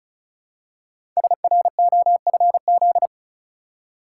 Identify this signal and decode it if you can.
Morse code, heard as a keyed tone, about 36 wpm, 700 Hz. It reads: SROFZ
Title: Code Groups - 5 Letters - 35wpm